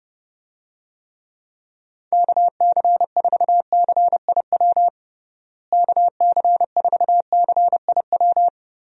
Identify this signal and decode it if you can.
Morse code, heard as a keyed tone, about 29 wpm, 700 Hz. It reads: KC4CIW KC4CIW
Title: USA Call Signs - 6 Characters - Repeated 2x - 30wpm